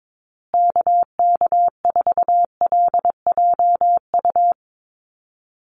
Morse code, heard as a keyed tone, about 22 wpm, 700 Hz. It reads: KK4LJU
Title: USA Call Signs Repeated 2x 22wpm